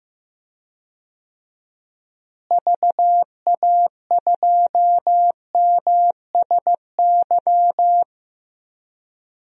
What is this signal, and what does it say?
Morse code, heard as a keyed tone, about 15 wpm, 700 Hz. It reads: VA2MSY
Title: DXCC-100 Call Signs - 6 Characters - Repeated 3x - 15wpm